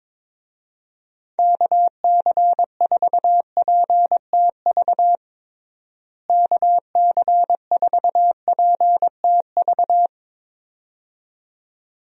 Morse code, heard as a keyed tone, about 22 wpm, 700 Hz. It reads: KC4PTV KC4PTV
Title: USA Call Signs - 6 Characters - Repeated 2x - 22wpm